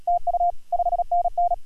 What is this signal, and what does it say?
Morse code, heard as a keyed tone, about 38 wpm, 700 Hz. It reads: TU 5NN
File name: AUDIO2019 / HG1G CW